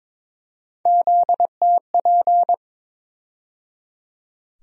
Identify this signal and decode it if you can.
Morse code, heard as a keyed tone, about 22 wpm, 700 Hz. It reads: ZTP
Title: USA Call Sign Postfixes - 3 Letters 22wpm